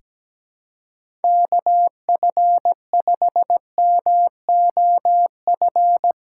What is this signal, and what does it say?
Morse code, heard as a keyed tone, about 17 wpm, 700 Hz. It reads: KF5MOF
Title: USA Callsign DE Callsign - 6 Characters - 17wpm